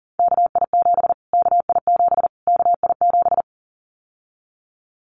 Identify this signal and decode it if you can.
Morse code, heard as a keyed tone, about 40 wpm, 700 Hz. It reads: KI7 KI7 KI7
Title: Callsign Prefixes for Top 25 DXCC - Repeated 3x - 40wpm